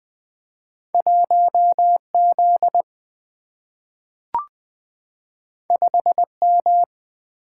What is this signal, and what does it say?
Morse code, heard as a keyed tone, about 20 wpm, 700 Hz. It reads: 1Z 5M
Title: Two Letter-Number Combinations - 20wpm